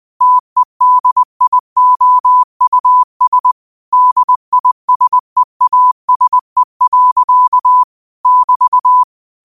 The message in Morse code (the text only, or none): TEDIOUS DISEASE. <BT>